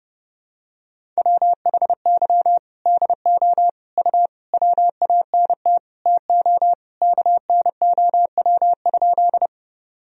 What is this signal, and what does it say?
Morse code, heard as a keyed tone, about 30 wpm, 700 Hz. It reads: WHY DO U WANT TO KNOW?
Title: Sentences from Top 200 Words and Abbreviations 30wpm